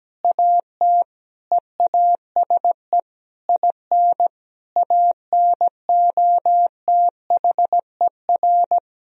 Morse code, heard as a keyed tone, about 17 wpm, 700 Hz. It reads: AT EASE IN ANOTHER
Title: Sentences from Top 1000 Words - Review Part 1 - 17wpm